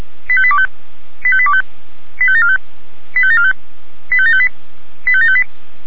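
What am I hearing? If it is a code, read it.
Five-tone selective call (ZVEI), heard as quick runs of five tones, tones of about 70 ms. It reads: 85825 85825 85745 85745 85758 85758
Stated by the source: ZVEI five-tone call received on SDR.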